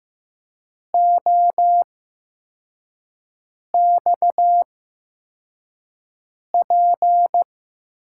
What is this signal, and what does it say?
Morse code, heard as a keyed tone, about 15 wpm, spaced as for 4 wpm, 700 Hz. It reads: OXP
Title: USA Call Signs - 8x Character Spacing - 15wpm